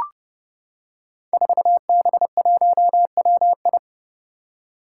Morse code, heard as a keyed tone, about 30 wpm, 700 Hz. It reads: 4B1WI
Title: Five Challenging Letter-Number Combinations 30wpm